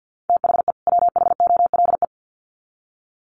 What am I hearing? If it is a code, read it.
Morse code, heard as a keyed tone, about 50 wpm, 700 Hz. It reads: THE WHOLE
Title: Sets of 2 Words 50wpm